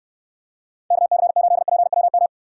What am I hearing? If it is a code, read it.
Morse code, heard as a keyed tone, about 72 wpm, 700 Hz. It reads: KC8PWM